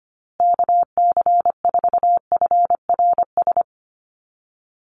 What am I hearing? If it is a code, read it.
Morse code, heard as a keyed tone, about 25 wpm, 700 Hz. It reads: KC4FRS
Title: USA Call Signs Repeated 3x 25wpm